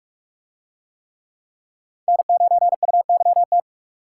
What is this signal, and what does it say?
Morse code, heard as a keyed tone, about 42 wpm, 700 Hz. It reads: N9UYT